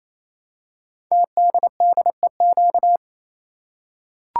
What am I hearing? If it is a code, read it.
Morse code, heard as a keyed tone, about 27 wpm, 700 Hz. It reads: TDDEQ
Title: Code Groups - 5 Letters - 28wpm